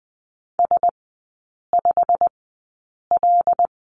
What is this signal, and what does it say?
Morse code, heard as a keyed tone, about 20 wpm, spaced as for 8 wpm, 700 Hz. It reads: S5L
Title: 093 Lesson 16e - Letter c - Three Characters 20wpm